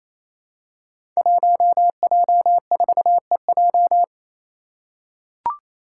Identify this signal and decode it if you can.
Morse code, heard as a keyed tone, about 28 wpm, 700 Hz. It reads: 1J4EJ
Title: Five Challenging Letter-Number Combinations 28wpm